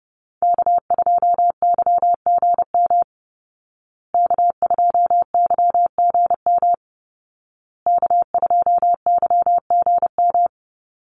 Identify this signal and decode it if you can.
Morse code, heard as a keyed tone, about 30 wpm, 700 Hz. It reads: K2YGM K2YGM K2YGM
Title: Call Signs Repeated 2x - 30wpm